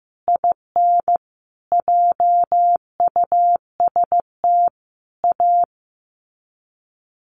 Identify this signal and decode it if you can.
Morse code, heard as a keyed tone, about 15 wpm, 700 Hz. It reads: IN JUST A
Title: Sets of 5 Words Encore 15wpm